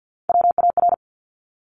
Morse code, heard as a keyed tone, about 50 wpm, 700 Hz. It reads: WAR
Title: Top Three Letter Combinations in Words 50wpm